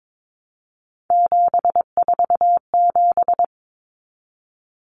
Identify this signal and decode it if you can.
Morse code, heard as a keyed tone, about 22 wpm, 700 Hz. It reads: 747
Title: Common Cut Numbers - 3-Digit 22wpm